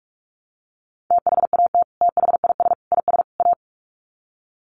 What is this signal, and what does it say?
Morse code, heard as a keyed tone, about 44 wpm, 700 Hz. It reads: THAT THIS IS A
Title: Sets of 4 Words 45wpm